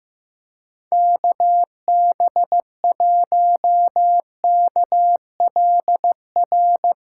Morse code, heard as a keyed tone, about 15 wpm, 700 Hz. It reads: KB1KLR
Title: USA Callsign DE Callsign - 6 Characters - 15wpm